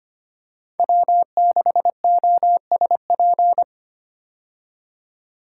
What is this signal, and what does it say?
Morse code, heard as a keyed tone, about 25 wpm, 700 Hz. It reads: W6OSP
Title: K1USN SST QSO Practice - 25wpm